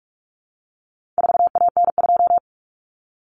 Morse code, heard as a keyed tone, about 44 wpm, 700 Hz. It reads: 4AN2
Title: Common Cut Numbers - 4-Digit 45wpm